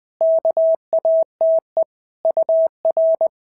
Morse code, heard as a keyed tone, about 20 wpm, 650 Hz. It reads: KATE UR
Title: POTA QSO Practice - 20wpm